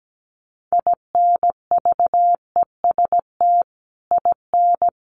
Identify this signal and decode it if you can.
Morse code, heard as a keyed tone, about 17 wpm, 700 Hz. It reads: INVEST IN
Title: Sentences from Top 1100 Words 17wpm